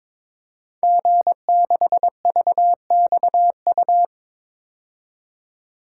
Morse code, heard as a keyed tone, about 22 wpm, 700 Hz. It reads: G6VXU
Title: Code Groups - 5 Letter:Numbers - 22wpm